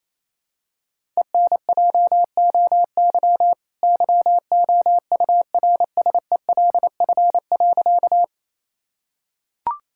Morse code, heard as a keyed tone, about 28 wpm, 700 Hz. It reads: ENJOY YOURSELF.
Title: Sentences from Top 1000 Words - Review Part 5 - 28wpm